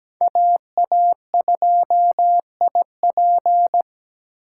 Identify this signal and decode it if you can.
Morse code, heard as a keyed tone, about 17 wpm, 700 Hz. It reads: AA2IP
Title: USA Call Signs - 5 Characters - 4x Character Spacing - 17wpm